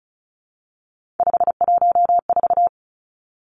Morse code, heard as a keyed tone, about 36 wpm, 700 Hz. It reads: 514
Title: Three Challenging Letter-Number Combinations 35wpm